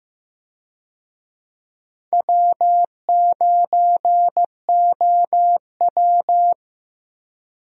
Morse code, heard as a keyed tone, about 15 wpm, 700 Hz. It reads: W9OW
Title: USA Call Signs - 4 Characters - Repeated 2x - 15wpm